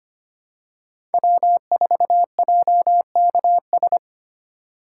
Morse code, heard as a keyed tone, about 25 wpm, 700 Hz. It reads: W4JKS
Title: USA Call Signs - 5 Characters - 3x Character Spacing - 25wpm